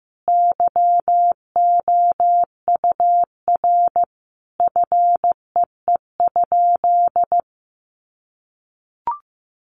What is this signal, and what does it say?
Morse code, heard as a keyed tone, about 15 wpm, 700 Hz. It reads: YOUR FEE?
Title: Sentences from Top 1500 Words 15wpm